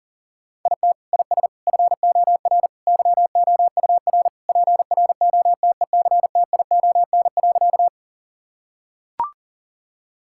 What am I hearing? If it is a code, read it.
Morse code, heard as a keyed tone, about 37 wpm, 700 Hz. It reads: IT IS FOR YOUR PROTECTION.
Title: Sentences from Top 1600 Words 40wpm